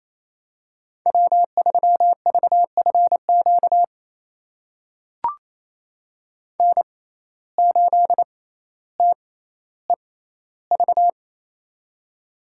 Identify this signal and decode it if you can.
Morse code, heard as a keyed tone, about 27 wpm, spaced as for 14 wpm, 700 Hz. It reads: W3VFQ N8TEV
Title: USA Call Signs - 5 Characters - 6x Character Spacing - 28wpm